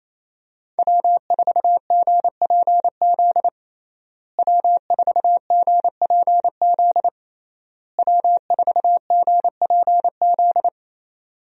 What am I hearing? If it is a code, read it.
Morse code, heard as a keyed tone, about 28 wpm, 700 Hz. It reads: W4GPZ W4GPZ W4GPZ
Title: USA Call Signs - 5 Characters - Repeated 3x - 28wpm